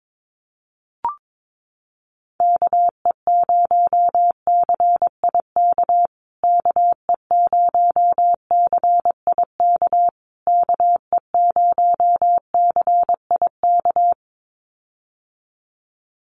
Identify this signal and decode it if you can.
Morse code, heard as a keyed tone, about 22 wpm, 700 Hz. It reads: KE0CIK KE0CIK KE0CIK
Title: USA Call Signs Repeated 3x 22wpm